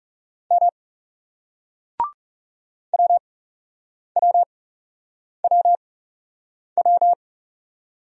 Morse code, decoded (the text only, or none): M WWWW